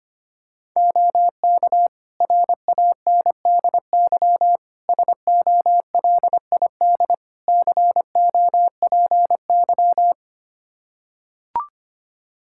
Morse code, heard as a keyed tone, about 25 wpm, 700 Hz. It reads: OK RANDY SOLID COPY
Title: Practice QSO - Exchange 3 - More Info - 25wpm